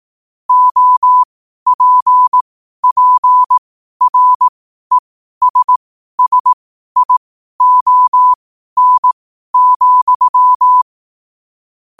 Morse code, heard as a keyed tone, about 18 wpm, spaced as for 12 wpm, 1000 Hz. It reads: OPPRESSION,
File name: qotdfiles / 12wpm